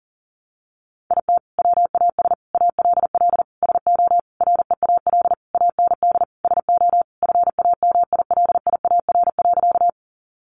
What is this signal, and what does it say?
Morse code, heard as a keyed tone, about 40 wpm, 700 Hz. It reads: IT WAS ALL SO REAL AND SO FAMILIAR.
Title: Sentences from Top 1000 Words 40wpm